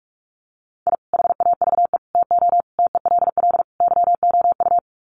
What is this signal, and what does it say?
Morse code, heard as a keyed tone, about 44 wpm, 700 Hz. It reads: I HAVE TO TELL YOU
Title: Sets of 5 Words 45wpm